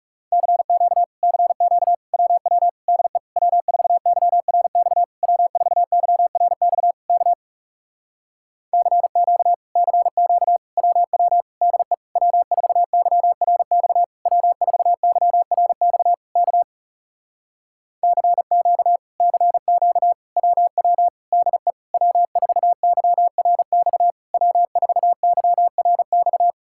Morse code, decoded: CQ CQ WW DE W4YRX W4YRX K CQ CQ WW DE W4YRX W4YRX K CQ CQ WW DE W4YRX W4YRX

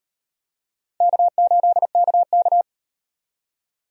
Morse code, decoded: K8KK

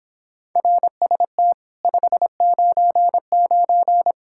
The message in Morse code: RST 599

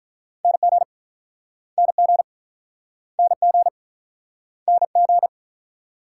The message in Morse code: NG NG NG NG

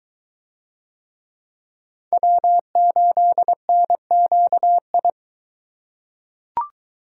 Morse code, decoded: W8NQI

